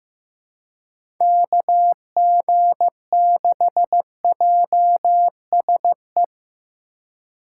Morse code, heard as a keyed tone, about 15 wpm, 700 Hz. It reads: KG6JSE
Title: USA Call Signs - 6 Characters - 3x Character Spacing - 15wpm